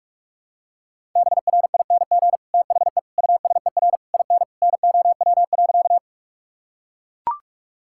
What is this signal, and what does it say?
Morse code, heard as a keyed tone, about 42 wpm, 700 Hz. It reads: BRING THE USER IN NOW.
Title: Sentences from Top 1000 Words - Review Part 5 - 45wpm